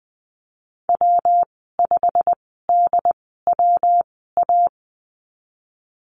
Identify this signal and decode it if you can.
Morse code, heard as a keyed tone, about 20 wpm, spaced as for 15 wpm, 700 Hz. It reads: W5DWA
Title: 122 Lesson 20g - Number 3 - Call Signs 20wpm